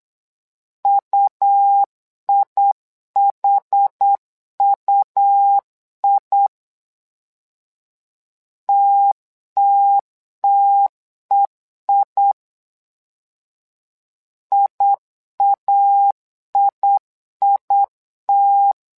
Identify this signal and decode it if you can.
Morse code, heard as a keyed tone, about 8 wpm, 800 Hz. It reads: UIHUI TTTEI IAIIT
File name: LEZIONI / 2 UHDN / 2 UH 8wpm 2129